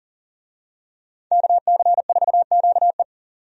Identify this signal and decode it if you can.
Morse code, heard as a keyed tone, about 40 wpm, 700 Hz. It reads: KC4QE
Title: USA Call Signs - 5 Characters - 40wpm